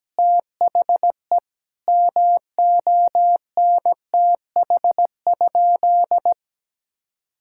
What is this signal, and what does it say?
Morse code, heard as a keyed tone, about 17 wpm, 700 Hz. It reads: THE MONTH?
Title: Sentences from Top 1000 Words - Review Part 6 - 17wpm